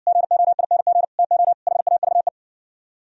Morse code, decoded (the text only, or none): GOING TO HAVE